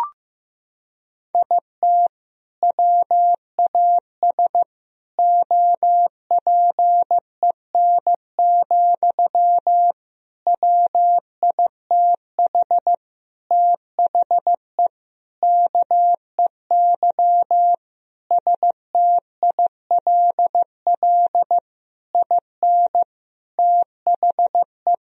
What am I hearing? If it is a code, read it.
Morse code, heard as a keyed tone, about 15 wpm, 700 Hz. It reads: IT WAS OPEN, WITH THE KEY STILL IN THE
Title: Sentences from Top 1000 Words - Review Part 4 - 15wpm